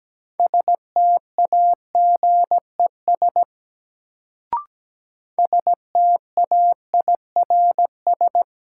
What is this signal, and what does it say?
Morse code, heard as a keyed tone, about 17 wpm, 700 Hz. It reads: STAGES STAIRS
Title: Sets of 5 Words Encore - New Word Drill - 17wpm